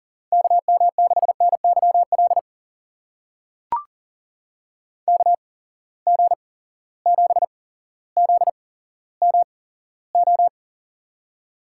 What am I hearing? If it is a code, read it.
Morse code, heard as a keyed tone, about 40 wpm, spaced as for 17 wpm, 700 Hz. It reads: KM6NYL KG7ZMO
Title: USA Call Signs - 6 Characters - 8x Character Spacing - 40wpm